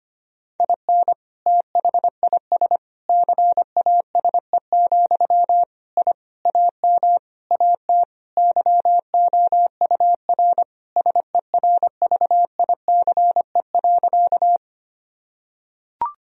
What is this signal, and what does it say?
Morse code, heard as a keyed tone, about 25 wpm, 700 Hz. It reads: IN THIS CASE, I AM AT YOUR SERVICE.
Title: Sentences from Top 1000 Words - Review Part 4 - 25wpm